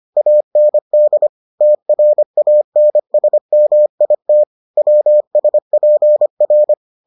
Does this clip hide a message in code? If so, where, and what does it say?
Yes: Morse code from 0.2 to 6.7 s, heard as a keyed tone, about 25 wpm, 600 Hz. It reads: AND TRANSMIT WSPR